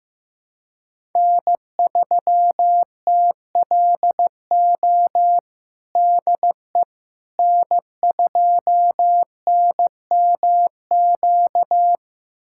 Morse code, heard as a keyed tone, about 15 wpm, 700 Hz. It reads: N3TLO DE N2NMQ
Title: USA Callsign DE Callsign - 5 Characters - 15wpm